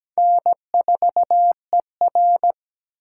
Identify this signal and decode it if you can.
Morse code, heard as a keyed tone, about 17 wpm, 700 Hz. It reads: N4ER
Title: USA Call Signs - 4 Characters - 3x Character Spacing - 17wpm